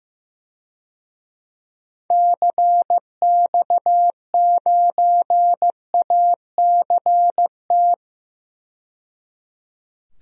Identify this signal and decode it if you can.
Morse code, heard as a keyed tone, about 15 wpm, 700 Hz. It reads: CX9ACT
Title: DXCC-100 Call Signs - 6 Characters - 15wpm